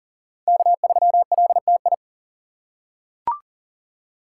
Morse code, heard as a keyed tone, about 38 wpm, 700 Hz. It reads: K3LTI
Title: USA Call Signs - 5 Characters - 6x Character Spacing - 40wpm